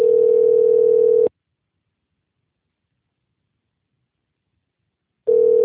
Ringback tone (North America), 440 + 480 Hz.